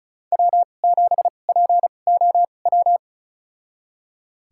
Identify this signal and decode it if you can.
Morse code, heard as a keyed tone, about 36 wpm, spaced as for 25 wpm, 700 Hz. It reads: W7POW
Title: USA Call Signs - 5 Characters - 2x Character Spacing - 35wpm